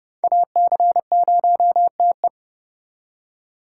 Morse code, heard as a keyed tone, about 37 wpm, 700 Hz. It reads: AC0TE
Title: Speed Racing - USA Call Signs 30wpm